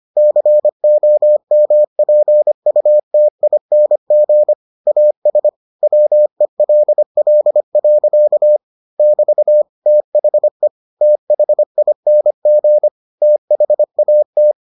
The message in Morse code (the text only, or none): COMPUTING AS WELL. <BT> THE THING THAT